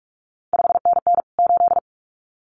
Morse code, heard as a keyed tone, about 46 wpm, 700 Hz. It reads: 5NN 8
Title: CQ WW DX Contest Exchange - 45wpm